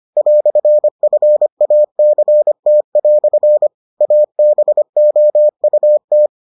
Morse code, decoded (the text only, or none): "FACT" ABOUT